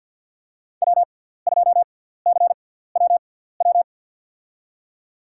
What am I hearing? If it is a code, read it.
Morse code, heard as a keyed tone, about 47 wpm, spaced as for 17 wpm, 700 Hz. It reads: W2CWW
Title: USA Call Signs - 6x Character Spacing - 50wpm